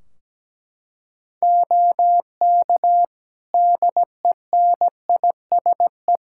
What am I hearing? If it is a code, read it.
Morse code, heard as a keyed tone, about 17 wpm, 700 Hz. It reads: OK DENISE
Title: K1USN SST QSO Practice 17wpm